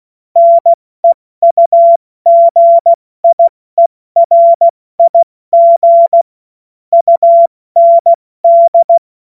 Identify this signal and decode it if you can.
Morse code, heard as a keyed tone, about 16 wpm, 680 Hz. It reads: NEUGIERIG UND
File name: Morsekurs / Witze / Witz 24. Januar 2023 70 F80